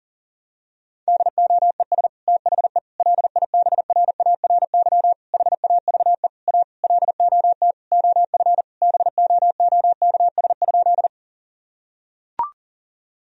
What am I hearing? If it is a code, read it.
Morse code, heard as a keyed tone, about 37 wpm, 700 Hz. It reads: DOES THE LIBRARY HAVE A LOT OF BOOKS?
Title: Sentences from Top 1000 Words - Review Part 4 - 40wpm